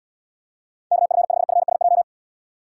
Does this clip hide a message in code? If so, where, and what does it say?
Yes: Morse code from 0.9 to 2.0 s, heard as a keyed tone, about 73 wpm, 700 Hz. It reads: KF5FSJ